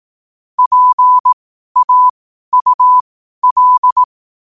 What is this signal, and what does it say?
Morse code, heard as a keyed tone, about 18 wpm, spaced as for 12 wpm, 1000 Hz. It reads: PAUL
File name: qotdfiles / 12wpm